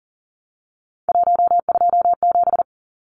Morse code, heard as a keyed tone, about 37 wpm, 700 Hz. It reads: 127
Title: Numbers - 3-Digit 40wpm